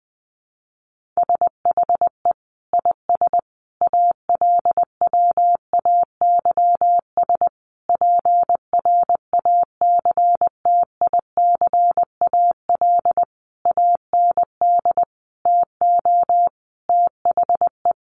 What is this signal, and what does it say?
Morse code, heard as a keyed tone, about 20 wpm, 700 Hz. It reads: SHE IS ALWAYS PRACTICAL AND TO THE
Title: Sentences from Top 1400 Words 20wpm